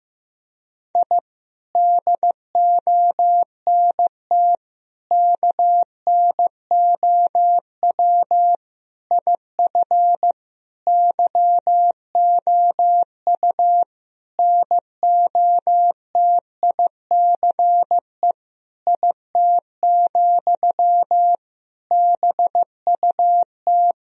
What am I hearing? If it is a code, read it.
Morse code, heard as a keyed tone, about 15 wpm, 700 Hz. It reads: I DONT KNOW IF YOU NOTICE IT, BUT
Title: Sentences from Top 1000 Words - Review Part 2 - 15wpm